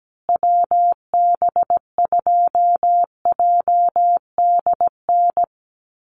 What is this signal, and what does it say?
Morse code, heard as a keyed tone, about 17 wpm, 700 Hz. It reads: WB2JDN
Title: USA Call Signs Repeated 2x 17wpm